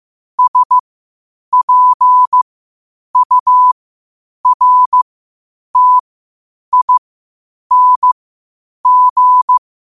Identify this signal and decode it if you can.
Morse code, heard as a keyed tone, about 15 wpm, spaced as for 8 wpm, 1000 Hz. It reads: SPURTING
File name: qotdfiles / 8wpm